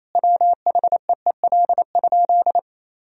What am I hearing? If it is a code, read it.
Morse code, heard as a keyed tone, about 27 wpm, 700 Hz. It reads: WHEEL?
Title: Sentences from Top 1000 Words - Review Part 3 - 28wpm